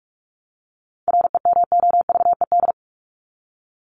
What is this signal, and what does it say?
Morse code, heard as a keyed tone, about 44 wpm, 700 Hz. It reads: REMOVED
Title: ED Suffix - Top 100 Words with Base Words 45wpm